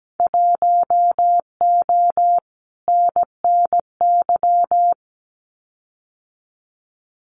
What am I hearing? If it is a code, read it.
Morse code, heard as a keyed tone, about 17 wpm, 700 Hz. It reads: 1O NNY
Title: Winter Field Day Exchanges 17wpm